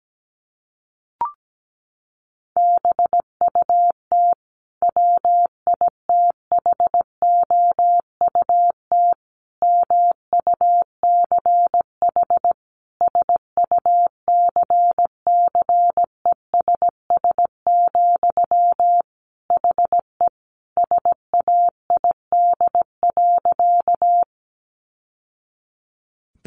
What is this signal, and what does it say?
Morse code, heard as a keyed tone, about 17 wpm, 700 Hz. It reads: BUT WITHOUT MUCH SUCCESS, HE SAID.
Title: Sentences from Top 900 Words 17wpm